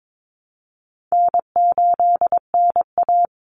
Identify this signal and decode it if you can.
Morse code, heard as a keyed tone, about 22 wpm, 700 Hz. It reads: N8NA
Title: Call Signs - 22wpm